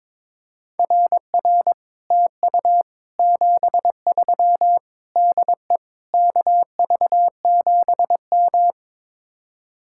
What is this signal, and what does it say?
Morse code, heard as a keyed tone, about 22 wpm, 700 Hz. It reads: RR TU 73 DE KV7M
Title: Speed-Racing Common QSO Phrases - 22wpm